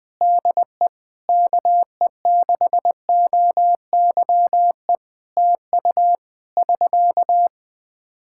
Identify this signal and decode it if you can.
Morse code, heard as a keyed tone, about 20 wpm, 700 Hz. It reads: DE KE6OYE TU <SK>
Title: Practice QSO - Exchange 4 - Ending - 20wpm